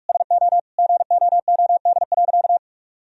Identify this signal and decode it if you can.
Morse code, heard as a keyed tone, about 47 wpm, 700 Hz. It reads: SO GOOD.
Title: Sentences from Top 1000 Words - Review Part 2 - 45wpm